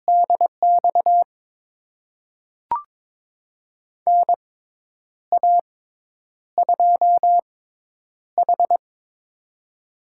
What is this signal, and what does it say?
Morse code, heard as a keyed tone, about 22 wpm, spaced as for 9 wpm, 700 Hz. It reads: DX NA2H
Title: USA Call Signs - 4 Characters - 6x Character Spacing - 22wpm